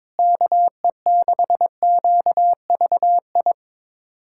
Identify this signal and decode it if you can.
Morse code, heard as a keyed tone, about 22 wpm, 700 Hz. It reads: KE6QVI
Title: USA Callsign DE Callsign - 6 Characters - 22wpm